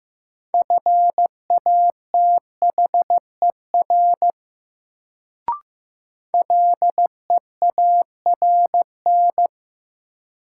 Morse code, heard as a keyed tone, about 15 wpm, 700 Hz. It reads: FATHER LEARN
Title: Sentences from Top 300 Words - New Word Drill - 15wpm